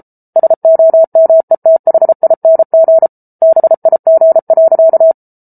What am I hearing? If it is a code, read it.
Morse code, heard as a keyed tone, about 34 wpm, 650 Hz. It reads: SOMETHING BIG.